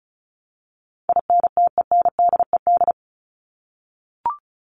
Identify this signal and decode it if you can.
Morse code, heard as a keyed tone, about 34 wpm, 700 Hz. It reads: INTENDED
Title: ED Suffix - Top 100 Words with Base Words 35wpm